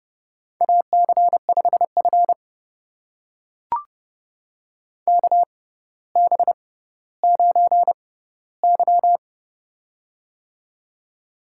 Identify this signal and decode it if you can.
Morse code, heard as a keyed tone, about 30 wpm, spaced as for 15 wpm, 700 Hz. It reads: AC5F KB9Y
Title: USA Call Signs - 4 Characters - 6x Character Spacing - 30wpm